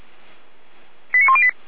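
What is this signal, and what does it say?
Five-tone selective call (ZVEI), heard as quick runs of five tones, tones of about 70 ms. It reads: 89108
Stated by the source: ZVEI five-tone call received by SDR.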